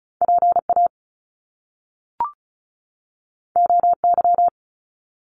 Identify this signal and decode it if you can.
Morse code, heard as a keyed tone, about 36 wpm, 700 Hz. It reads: PA OY
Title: Top Two Letter Combinations in Words 35wpm